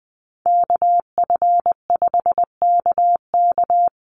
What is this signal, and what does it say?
Morse code, heard as a keyed tone, about 20 wpm, 700 Hz. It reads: KF5KK
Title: USA Call Signs Repeated 3x 20wpm